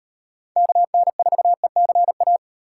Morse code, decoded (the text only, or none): KN4ECA